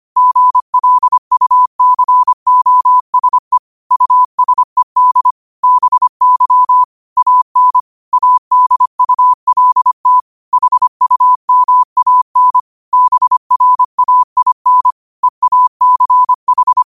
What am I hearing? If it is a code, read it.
Morse code, heard as a keyed tone, about 25 wpm, 1000 Hz. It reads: GLUCOSE USED BY AN ADULT HUMAN BRAIN EACH
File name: 25wpm